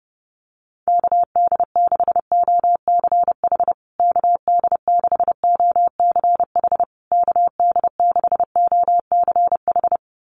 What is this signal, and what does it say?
Morse code, heard as a keyed tone, about 30 wpm, 700 Hz. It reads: KD6OCH KD6OCH KD6OCH
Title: USA Call Signs Repeated 3x 30wpm